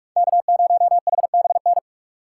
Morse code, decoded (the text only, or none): K0HBN